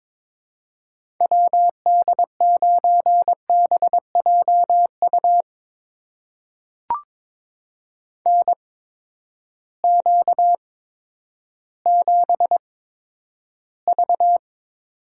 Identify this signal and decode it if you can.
Morse code, heard as a keyed tone, about 22 wpm, spaced as for 11 wpm, 700 Hz. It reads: WD9BJU NQ7V
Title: USA Call Signs - 8x Character Spacing - 22wpm